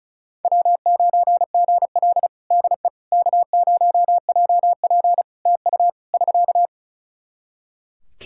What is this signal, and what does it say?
Morse code, heard as a keyed tone, about 36 wpm, 700 Hz. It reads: W9GL DE K0JP TU <SK>
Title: Practice QSO - Exchange 4 - Ending - 35wpm